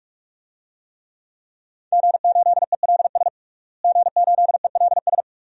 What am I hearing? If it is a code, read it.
Morse code, heard as a keyed tone, about 44 wpm, 700 Hz. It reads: G8ELS G8ELS
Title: DXCC-100 Call Signs - 5 Characters - Repeated 2x - 45wpm